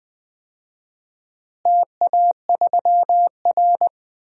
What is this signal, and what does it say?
Morse code, heard as a keyed tone, about 21 wpm, 700 Hz. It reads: TA3R
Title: DXCC-100 Call Signs - 4 Characters - Repeated 3x - 20wpm